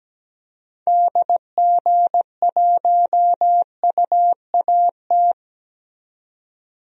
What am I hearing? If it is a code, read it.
Morse code, heard as a keyed tone, about 17 wpm, 700 Hz. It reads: DG1UAT
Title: DXCC-100 Call Signs - 6 Characters - 17wpm